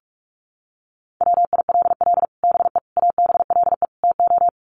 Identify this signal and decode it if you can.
Morse code, heard as a keyed tone, about 44 wpm, 700 Hz. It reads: WILL BE ABLE TO
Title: Sets of 4 Words 45wpm